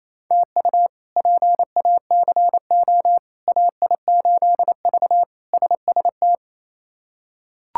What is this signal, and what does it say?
Morse code, heard as a keyed tone, about 28 wpm, 700 Hz. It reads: TU PACO AI8V SST
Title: K1USN SST QSO Practice - 28wpm